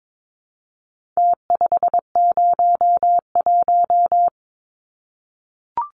Morse code, heard as a keyed tone, about 22 wpm, 700 Hz. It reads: T501
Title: Numbers - Contest Style - 4-Digit 22wpm